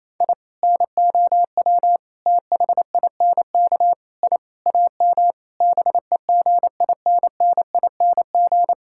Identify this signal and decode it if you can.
Morse code, heard as a keyed tone, about 28 wpm, 700 Hz. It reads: I NOW THINK I AM BEGINNING
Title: Sentences from Top 1000 Words - Review Part 5 - 28wpm